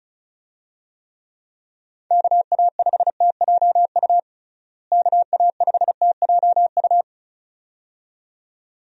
Morse code, heard as a keyed tone, about 34 wpm, 700 Hz. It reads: KA5TJU KA5TJU
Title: USA Call Signs - 6 Characters - Repeated 2x - 35wpm